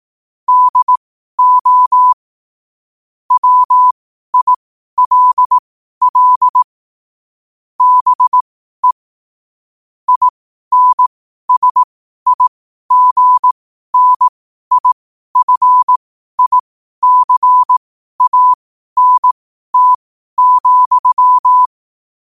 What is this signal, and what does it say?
Morse code, heard as a keyed tone, about 18 wpm, spaced as for 12 wpm, 1000 Hz. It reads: DO WILL BE INSIGNIFICANT,